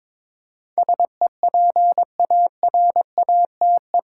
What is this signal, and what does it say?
Morse code, heard as a keyed tone, about 22 wpm, 700 Hz. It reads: SEPARATE
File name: practice / 22wpm / Eight Letter Words - 22wpm